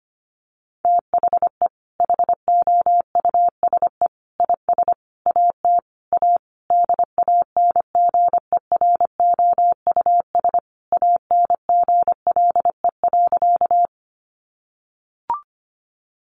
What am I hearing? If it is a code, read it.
Morse code, heard as a keyed tone, about 25 wpm, 700 Hz. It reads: THE HOUSE IS AT A DANGEROUS ANGLE.
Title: Sentences from Top 1500 Words 25wpm